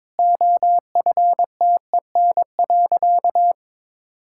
Morse code, heard as a keyed tone, about 22 wpm, 700 Hz. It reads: OFTEN.
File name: practice / 22wpm / Easy Sentences with Hints - 22wpm